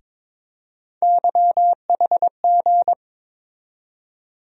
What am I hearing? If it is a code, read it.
Morse code, heard as a keyed tone, about 21 wpm, 700 Hz. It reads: YHG
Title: USA Call Sign Postfixes - 3 Letters 22wpm